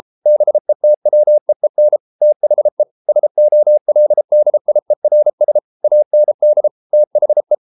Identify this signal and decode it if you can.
Morse code, heard as a keyed tone, about 33 wpm, 600 Hz. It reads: BETWEEN THE SOLDIERS AND THE